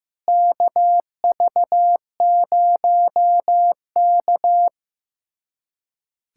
Morse code, heard as a keyed tone, about 15 wpm, 700 Hz. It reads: KV0K
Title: USA Callsign DE Callsign - 4 Characters - 15wpm